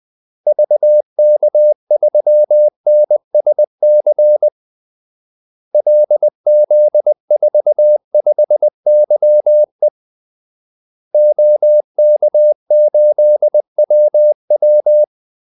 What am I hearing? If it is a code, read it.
Morse code, heard as a keyed tone, about 20 wpm, 600 Hz. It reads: VK3NSC LZ45YE OK8WW
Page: Kaldesignaler 20 wpm